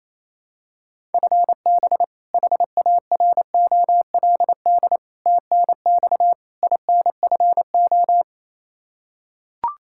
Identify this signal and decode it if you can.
Morse code, heard as a keyed tone, about 27 wpm, 700 Hz. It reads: FB HAROLD TNX INFO
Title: Practice QSO - Exchange 3 - More Info - 28wpm